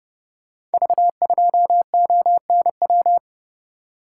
Morse code, heard as a keyed tone, about 30 wpm, 700 Hz. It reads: V2ONW
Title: Code Groups - 5 Letter:Numbers - 30wpm